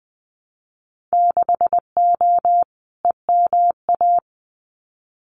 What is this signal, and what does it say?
Morse code, heard as a keyed tone, about 20 wpm, 700 Hz. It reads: 6O EMA
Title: Winter Field Day Exchanges Encore 20wpm